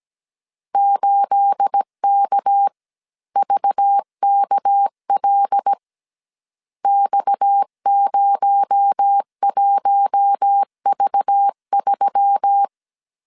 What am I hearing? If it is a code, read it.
Morse code, heard as a keyed tone, about 17 wpm, 800 Hz. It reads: 8KVKL X01V3
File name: CW / DET / 1 9 1500C S C 16wpm 2909